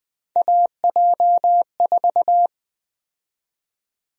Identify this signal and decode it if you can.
Morse code, heard as a keyed tone, about 20 wpm, 700 Hz. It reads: AJ4
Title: USA Call Sign Prefixes - Repeated 2x - 20wpm